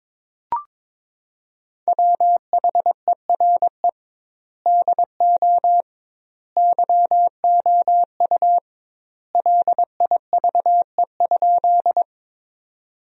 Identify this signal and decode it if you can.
Morse code, heard as a keyed tone, about 22 wpm, 700 Hz. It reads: WHERE DO YOU LIVE?
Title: 2x Word-Spacing - Sentences from Top 200 Words - 22wpm